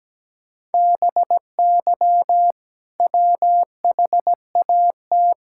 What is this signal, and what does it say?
Morse code, heard as a keyed tone, about 17 wpm, 700 Hz. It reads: BY WHAT
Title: Sentences from Top 1000 Words - Review Part 6 - 17wpm